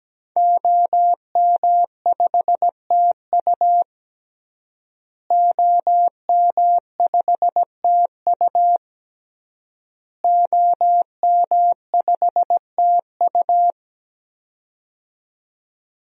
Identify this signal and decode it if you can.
Morse code, heard as a keyed tone, about 17 wpm, 700 Hz. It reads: OM5TU OM5TU OM5TU
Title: DXCC-100 Call Signs - 5 Characters - Repeated 3x - 17wpm